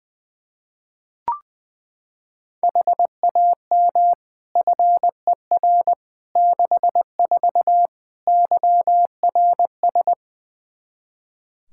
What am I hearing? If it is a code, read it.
Morse code, heard as a keyed tone, about 21 wpm, 700 Hz. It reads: HAM FER 64 YRS
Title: K1USN SST QSO Practice 20wpm